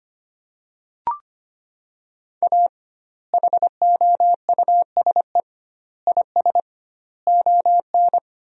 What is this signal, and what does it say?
Morse code, heard as a keyed tone, about 25 wpm, 700 Hz. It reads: A HOUSE IS ON
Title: 2x Word-Spacing - Sentences from Top 400 Words - 25wpm